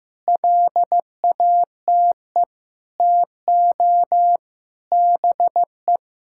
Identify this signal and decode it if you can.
Morse code, heard as a keyed tone, about 15 wpm, 700 Hz. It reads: LATE TO BE
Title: Sentences from Top 1600 Words 15wpm